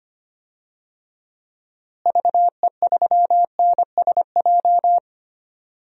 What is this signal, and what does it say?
Morse code, heard as a keyed tone, about 25 wpm, 700 Hz. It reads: VE3NSJ
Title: DXCC-100 Call Signs - 6 Characters - Repeated 3x - 25wpm